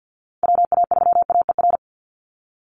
Morse code, heard as a keyed tone, about 48 wpm, 700 Hz. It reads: WA3AER
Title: Call Signs Repeated 3x - 50wpm